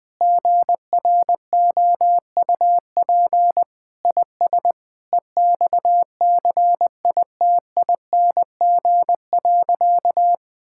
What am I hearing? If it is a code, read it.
Morse code, heard as a keyed tone, about 20 wpm, 700 Hz. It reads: GROUP IS EXCITING.
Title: Sentences with 4-Word Phrases - 20wpm